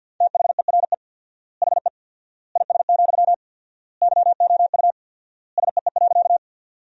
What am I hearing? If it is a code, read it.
Morse code, heard as a keyed tone, about 50 wpm, 700 Hz. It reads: THERE HE IS, YOU SEE.
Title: 4x Word-Spacing - Sentences from Top 100 Words - 50wpm